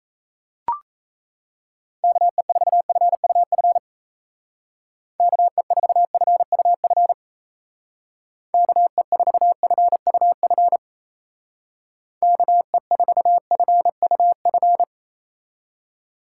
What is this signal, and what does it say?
Morse code, heard as a keyed tone, about 34 wpm, 700 Hz. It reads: KE4FUF KE4FUF KE4FUF KE4FUF
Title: Speed Racing - USA Call Signs 28wpm